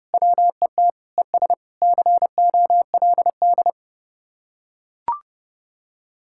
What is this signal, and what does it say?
Morse code, heard as a keyed tone, about 30 wpm, 700 Hz. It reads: WET ES COLD